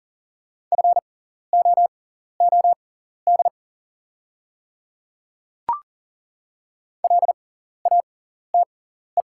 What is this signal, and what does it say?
Morse code, heard as a keyed tone, about 40 wpm, spaced as for 14 wpm, 700 Hz. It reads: FOOD LATE
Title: 6x Character Spacing - Top 300 Words - 40wpm